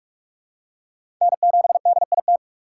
Morse code, heard as a keyed tone, about 46 wpm, 700 Hz. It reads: N7DIT
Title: USA Call Signs - 5 Characters - 3x Character Spacing - 45wpm